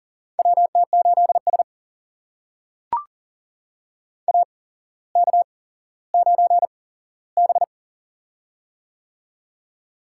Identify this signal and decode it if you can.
Morse code, heard as a keyed tone, about 38 wpm, spaced as for 17 wpm, 700 Hz. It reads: WT8S AK9B
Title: USA Call Signs - 4 Characters - 8x Character Spacing - 40wpm